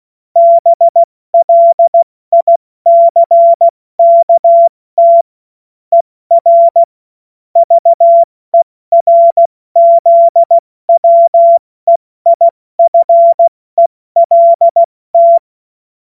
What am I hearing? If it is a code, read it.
Morse code, heard as a keyed tone, about 16 wpm, 680 Hz. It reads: BLICKT ER VERZWEIFELT
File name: Morsekurs / Witze / Witz 11. Januar 2022, 70 F80